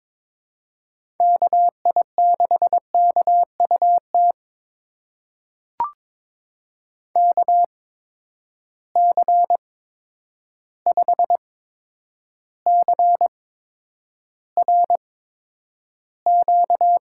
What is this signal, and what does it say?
Morse code, heard as a keyed tone, about 22 wpm, spaced as for 9 wpm, 700 Hz. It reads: KI6KUT KC5CRQ
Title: USA Call Signs - 6 Characters - 8x Character Spacing - 22wpm